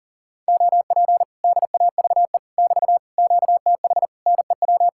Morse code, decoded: OP DAVE <BT> QTH NEW

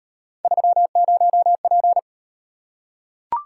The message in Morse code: 30P

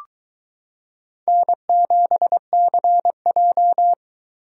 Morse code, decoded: N7CJ

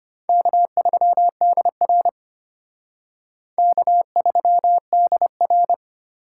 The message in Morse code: K3DR K3DR

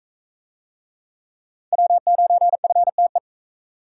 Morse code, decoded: W9FTE

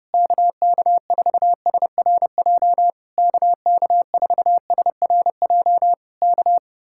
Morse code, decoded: KK4SRJ KK4SRJ K

